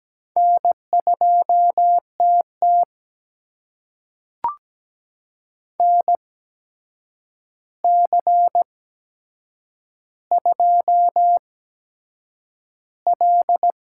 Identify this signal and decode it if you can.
Morse code, heard as a keyed tone, about 17 wpm, spaced as for 7 wpm, 700 Hz. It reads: N2TT NC2L
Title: USA Call Signs - 4 Characters - 8x Character Spacing - 17wpm